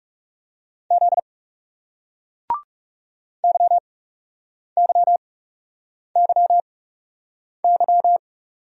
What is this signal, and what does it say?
Morse code, heard as a keyed tone, about 40 wpm, spaced as for 8 wpm, 700 Hz. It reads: Z YYYY